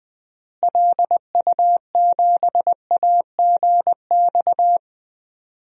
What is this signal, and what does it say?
Morse code, heard as a keyed tone, about 21 wpm, 700 Hz. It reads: LU7AGX
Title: DXCC-100 Call Signs - 6 Characters - Repeated 3x - 20wpm